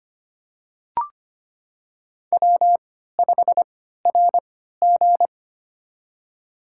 Morse code, heard as a keyed tone, about 25 wpm, spaced as for 14 wpm, 700 Hz. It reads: W5RG
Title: USA Call Signs - 3x Character Spacing - 25wpm